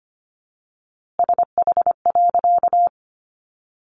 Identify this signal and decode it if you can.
Morse code, heard as a keyed tone, about 25 wpm, 700 Hz. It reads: SH.
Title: A Sound Of Thunder 25wpm